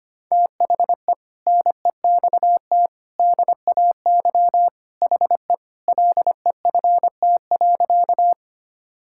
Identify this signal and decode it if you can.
Morse code, heard as a keyed tone, about 25 wpm, 700 Hz. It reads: THE NEXT DAY HE LEFT.
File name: practice / 25wpm / Sentences with 3-Word Phrases - 25wpm